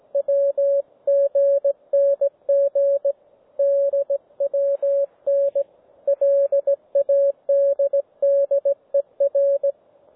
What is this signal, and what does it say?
Morse code, heard as a keyed tone, about 18 wpm, 560 Hz, with radio noise. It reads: WGNG DWN LADDER
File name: GW3UEP / G3VTT OneValveTX